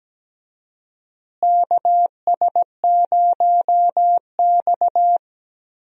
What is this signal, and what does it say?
Morse code, heard as a keyed tone, about 17 wpm, 700 Hz. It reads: KS0X
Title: USA Call Signs - 4 Characters - Repeated 2x - 17wpm